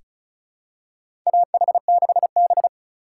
Morse code, heard as a keyed tone, about 36 wpm, 700 Hz. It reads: AH6B
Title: USA Call Signs - 4 Characters - 3x Character Spacing - 35wpm